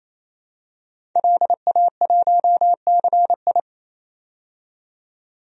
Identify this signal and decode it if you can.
Morse code, heard as a keyed tone, about 28 wpm, 700 Hz. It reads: LA1CI
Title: DXCC-100 Call Signs - 5 Characters - 28wpm